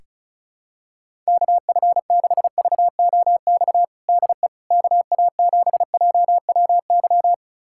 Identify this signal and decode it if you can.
Morse code, heard as a keyed tone, about 34 wpm, 700 Hz. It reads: KF6VOX DE KA7JWY
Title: USA Callsign DE Callsign - 6 Characters - 35wpm